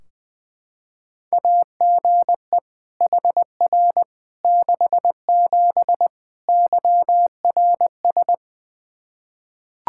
Morse code, heard as a keyed tone, about 20 wpm, 700 Hz. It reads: AGE HR 67 YRS
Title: Practice QSO - Exchange 3 - More Info - 20wpm